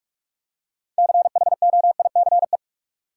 Morse code, heard as a keyed tone, about 47 wpm, 700 Hz. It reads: CHOICE
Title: Six Letter Words - 45wpm